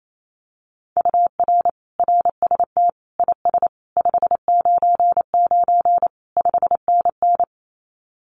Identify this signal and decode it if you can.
Morse code, heard as a keyed tone, about 28 wpm, 700 Hz. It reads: UR RST IS 599 5NN
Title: Common QSO Phrases Encore 28wpm